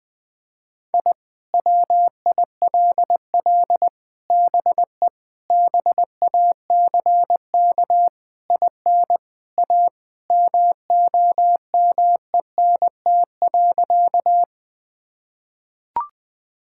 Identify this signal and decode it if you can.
Morse code, heard as a keyed tone, about 20 wpm, 700 Hz. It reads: I WILL BE BACK IN A MOMENT.
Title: Sentences with 5-Word Phrases - Spoken Beginning - 20wpm